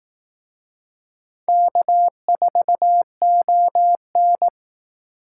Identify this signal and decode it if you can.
Morse code, heard as a keyed tone, about 19 wpm, 700 Hz. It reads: K4ON